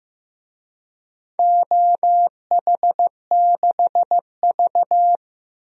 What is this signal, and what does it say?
Morse code, heard as a keyed tone, about 15 wpm, 700 Hz. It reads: OH6V
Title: DXCC-100 Call Signs - 4 Characters - Repeated 2x - 15wpm